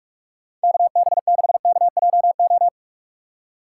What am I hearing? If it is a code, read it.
Morse code, heard as a keyed tone, about 57 wpm, 700 Hz. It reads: KB6KJO